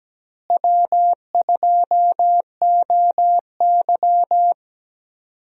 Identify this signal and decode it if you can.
Morse code, heard as a keyed tone, about 17 wpm, 700 Hz. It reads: W2OY
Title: USA Call Signs - 4 Characters - Repeated 3x - 17wpm